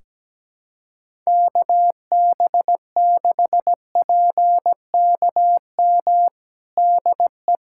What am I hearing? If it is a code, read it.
Morse code, heard as a keyed tone, about 17 wpm, 700 Hz. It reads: KB6PKM DE
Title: Practice QSO - Exchange 4 - Ending - 17wpm